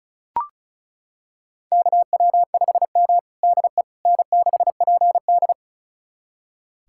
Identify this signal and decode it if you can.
Morse code, heard as a keyed tone, about 36 wpm, 700 Hz. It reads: KW5M DE N6PD
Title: USA Callsign DE Callsign - 4 Characters - 35wpm